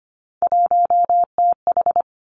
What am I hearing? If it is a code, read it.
Morse code, heard as a keyed tone, about 25 wpm, 700 Hz. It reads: 1TH
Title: Three Challenging Letter-Number Combinations 25wpm